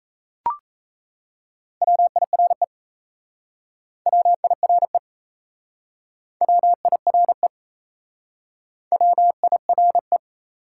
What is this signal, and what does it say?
Morse code, heard as a keyed tone, about 38 wpm, 700 Hz. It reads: WIRE WIRE WIRE WIRE